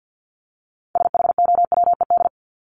Morse code, heard as a keyed tone, about 50 wpm, 700 Hz. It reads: SHOWED
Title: ED Suffix - Top 100 Words with Base Words 50wpm